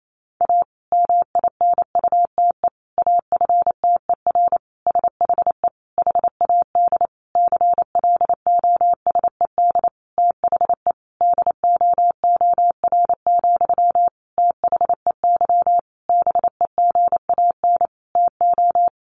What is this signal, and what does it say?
Morse code, heard as a keyed tone, about 28 wpm, 700 Hz. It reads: A MINUTE AFTER SHE HAD CLOSED THE DOOR, THEY BEGAN TO
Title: Sentences from Top 1400 Words 28wpm